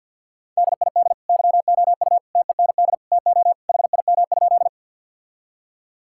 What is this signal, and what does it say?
Morse code, heard as a keyed tone, about 52 wpm, 700 Hz. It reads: DID YOU TEND TO HIM?
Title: Sentences from Top 1000 Words - Review Part 5 - 50wpm